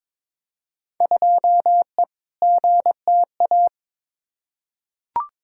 Morse code, heard as a keyed tone, about 22 wpm, 700 Hz. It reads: 2E GTA
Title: Field Day Exchanges Encore - 22wpm